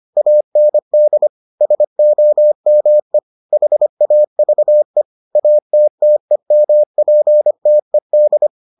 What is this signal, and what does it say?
Morse code, heard as a keyed tone, about 25 wpm, 600 Hz. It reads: AND SOME HAVE ATTEMPTED